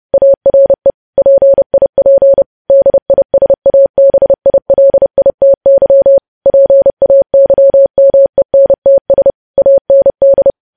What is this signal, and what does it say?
Morse code, heard as a keyed tone, about 30 wpm, 550 Hz. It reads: ARE PIP DISABILITY PAYMENTS AND
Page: news headlines in morse code at 30 wpm